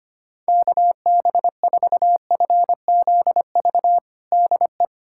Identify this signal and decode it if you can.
Morse code, heard as a keyed tone, about 25 wpm, 700 Hz. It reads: KB4FZV DE
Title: USA Callsign DE Callsign - 6 Characters - 25wpm